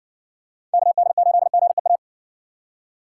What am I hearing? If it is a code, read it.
Morse code, heard as a keyed tone, about 61 wpm, 700 Hz. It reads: KD8GEA